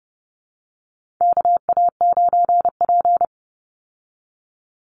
Morse code, heard as a keyed tone, about 30 wpm, 700 Hz. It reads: KA9P
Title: Call Signs Repeated 2x - 30wpm